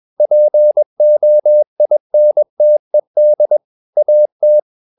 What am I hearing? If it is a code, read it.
Morse code, heard as a keyed tone, about 21 wpm, 600 Hz. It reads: POINTED AT
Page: CW med Gnister